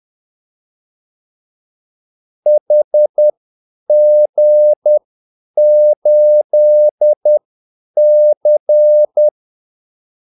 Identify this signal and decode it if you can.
Morse code, heard as a keyed tone, about 10 wpm, spaced as for 8 wpm, 600 Hz. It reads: HG8C
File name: Callsigns-10-8wpm